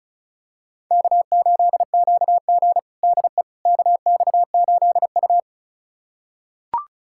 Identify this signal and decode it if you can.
Morse code, heard as a keyed tone, about 36 wpm, 690 Hz. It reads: K8QG DE KX8U
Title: USA Callsign DE Callsign - 4 Characters - 35wpm